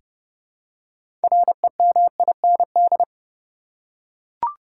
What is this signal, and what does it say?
Morse code, heard as a keyed tone, about 30 wpm, 700 Hz. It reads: REMIND
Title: Six Letter Words - 30wpm